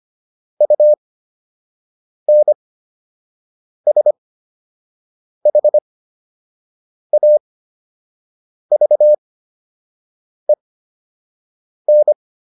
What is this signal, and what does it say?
Morse code, heard as a keyed tone, about 25 wpm, spaced as for 6 wpm, 600 Hz. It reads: UNSHAVEN